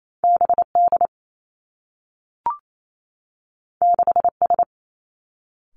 Morse code, heard as a keyed tone, about 28 wpm, 700 Hz. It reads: B D 6 S
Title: Two Challenging Letter-Number Combinations 28wpm